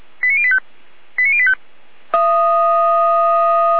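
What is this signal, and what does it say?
Five-tone selective call (ZVEI), heard as quick runs of five tones, tones of about 70 ms. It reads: 89085 89085
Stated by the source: ZVEI five-tone selective call 89510 received by SDR.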